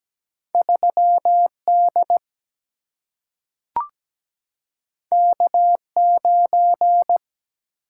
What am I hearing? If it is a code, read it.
Morse code, heard as a keyed tone, about 17 wpm, 700 Hz. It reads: 3D K9
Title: Single-Letter Single-Number 17wpm